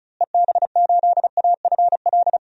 Morse code, heard as a keyed tone, about 36 wpm, 700 Hz. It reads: EB8AFL
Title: DXCC-100 Call Signs - 6 Characters - Repeated 2x - 35wpm